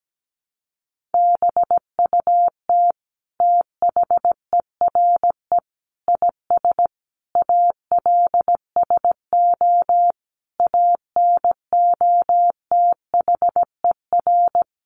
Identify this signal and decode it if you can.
Morse code, heard as a keyed tone, about 17 wpm, 700 Hz. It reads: BUT THERE IS ALSO ANOTHER
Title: Sentences from Top 1500 Words 17wpm